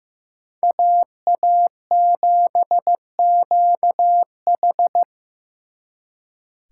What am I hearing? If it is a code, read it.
Morse code, heard as a keyed tone, about 18 wpm, 700 Hz. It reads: AA7QH
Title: Speed Racing - USA 2x2 Call Signs 15wpm